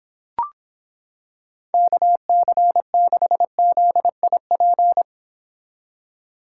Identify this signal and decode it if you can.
Morse code, heard as a keyed tone, about 26 wpm, 700 Hz. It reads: KC6ZIP